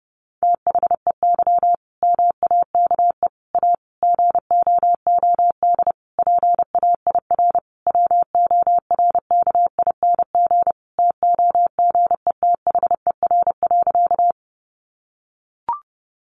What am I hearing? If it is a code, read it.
Morse code, heard as a keyed tone, about 30 wpm, 700 Hz. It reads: THEY MAKE A GOOD PAIR WORKING TOGETHER.
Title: Sentences from Top 1200 Words 30wpm